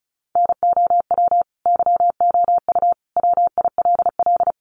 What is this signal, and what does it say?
Morse code, heard as a keyed tone, about 34 wpm, 700 Hz. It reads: NOW YOU WILL
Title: Sentences from Top 1200 Words 35wpm